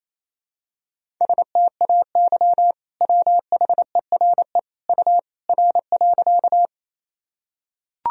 Morse code, heard as a keyed tone, about 28 wpm, 700 Hz. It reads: STAY WHERE U R.
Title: Sentences from Top 500 Words and Abbreviations 28wpm